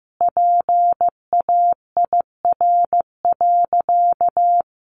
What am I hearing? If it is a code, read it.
Morse code, heard as a keyed tone, about 14 wpm, 700 Hz. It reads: PAIR.
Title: Sentences from Top 1200 Words 15wpm